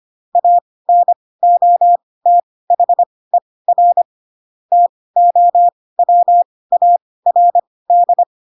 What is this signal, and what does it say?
Morse code, heard as a keyed tone, about 25 wpm, spaced as for 18 wpm, 700 Hz. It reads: ANOTHER TOWARD
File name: Pinocchio / Pinocchio ch4 12 25 ik1hge